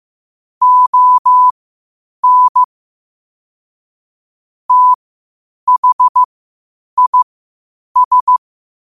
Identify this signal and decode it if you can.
Morse code, heard as a keyed tone, about 15 wpm, spaced as for 8 wpm, 1000 Hz. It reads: ON THIS
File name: qotdfiles / 8wpm